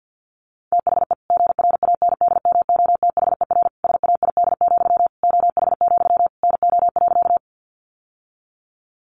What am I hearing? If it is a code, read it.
Morse code, heard as a keyed tone, about 50 wpm, 700 Hz. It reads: THE GRANDMOTHER SAID, OH, NO.
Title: Sentences from Top 1200 Words 50wpm